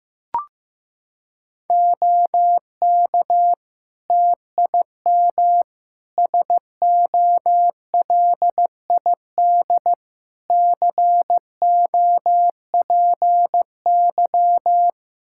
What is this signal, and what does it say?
Morse code, heard as a keyed tone, about 15 wpm, 700 Hz. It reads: OK TIM SOLID COPY
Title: Practice QSO - Exchange 3 - More Info - 15wpm